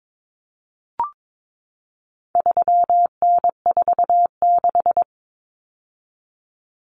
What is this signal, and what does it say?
Morse code, heard as a keyed tone, about 22 wpm, 700 Hz. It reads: 3N46
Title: Common Cut Numbers - 4-Digit 22wpm